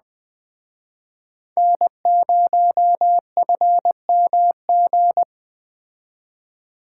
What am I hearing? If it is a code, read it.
Morse code, heard as a keyed tone, about 20 wpm, 700 Hz. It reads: N0FMG
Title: Speed Racing - USA 1x3 Call Signs 17wpm